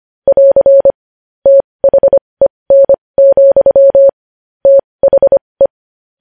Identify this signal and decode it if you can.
Morse code, heard as a keyed tone, about 25 wpm, spaced as for 20 wpm, 550 Hz. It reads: <AR> THEN, THE
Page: news-headlines-in-morse-code-at-20-wpm